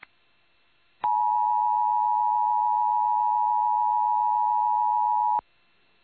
Emergency Alert System (SAME) tones (the 853 and 960 Hz attention tone). It is heard from 1.0 to 5.5 s.